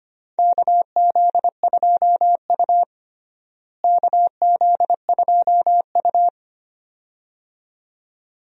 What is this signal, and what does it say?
Morse code, heard as a keyed tone, about 25 wpm, 700 Hz. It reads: KZ2U KZ2U
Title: USA Call Signs - 4 Characters - Repeated 2x - 25wpm